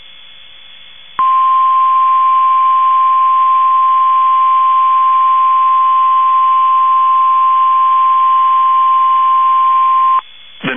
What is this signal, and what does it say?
Emergency Alert System (SAME) tones (the 1050 Hz attention tone). It is heard from 1.2 to 10.2 s.
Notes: emitido por sistema NOAA - Radio KEC50 por 162,475 Mhz